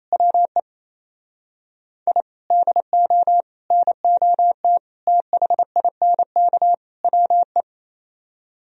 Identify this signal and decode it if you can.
Morse code, heard as a keyed tone, about 37 wpm, 700 Hz. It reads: WE I DO NOT THINK WE